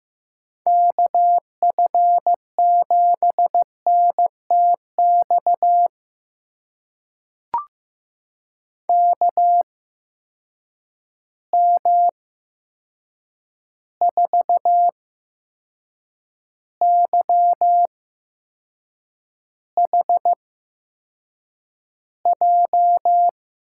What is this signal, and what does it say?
Morse code, heard as a keyed tone, about 15 wpm, spaced as for 6 wpm, 700 Hz. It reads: KF7NTX KM4YHJ
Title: USA Call Signs - 8x Character Spacing - 15wpm